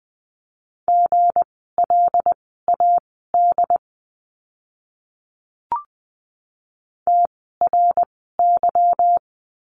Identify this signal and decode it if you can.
Morse code, heard as a keyed tone, about 20 wpm, spaced as for 15 wpm, 700 Hz. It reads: GLAD TRY
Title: 160 Lesson 26f - Letter g - Words 20wpm